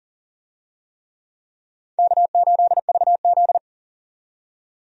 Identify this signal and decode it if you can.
Morse code, heard as a keyed tone, about 40 wpm, 700 Hz. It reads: K8VZ
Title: USA Call Signs - 4 Characters - 40wpm